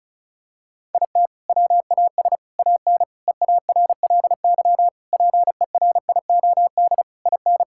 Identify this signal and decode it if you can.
Morse code, heard as a keyed tone, about 36 wpm, 700 Hz. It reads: IT WAS AN EARLY PERIOD IN
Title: Sentences from Top 1000 Words - Review Part 5 - 35wpm